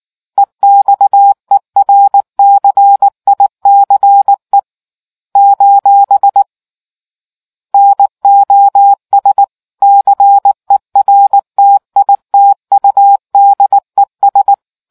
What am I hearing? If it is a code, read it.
Morse code, heard as a keyed tone, about 19 wpm, 800 Hz. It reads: EXERCICE : NOS CERTITUDES